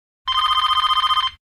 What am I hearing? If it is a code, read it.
There is no code.